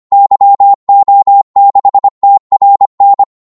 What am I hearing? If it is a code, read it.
Morse code, heard as a keyed tone, about 25 wpm, 800 Hz. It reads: YO6TRN
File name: Morse / qso-25wpm